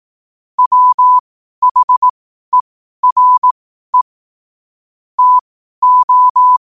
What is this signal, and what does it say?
Morse code, heard as a keyed tone, about 18 wpm, spaced as for 12 wpm, 1000 Hz. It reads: WHERE TO
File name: qotdfiles / 12wpm